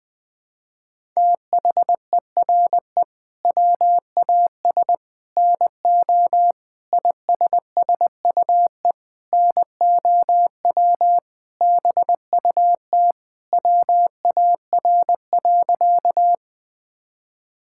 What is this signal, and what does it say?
Morse code, heard as a keyed tone, about 20 wpm, 700 Hz. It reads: THERE WAS NO ISSUE NOW BUT WAR.
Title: Sentences from Top 1000 Words - Review Part 6 - 20wpm